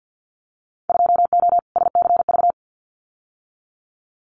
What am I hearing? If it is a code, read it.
Morse code, heard as a keyed tone, about 50 wpm, 700 Hz. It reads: 2O SCV
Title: Winter Field Day Exchanges Encore 50wpm